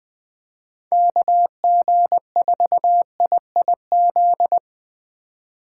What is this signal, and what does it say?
Morse code, heard as a keyed tone, about 21 wpm, 700 Hz. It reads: KG4IIZ
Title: USA Call Signs - 6 Characters - 3x Character Spacing - 20wpm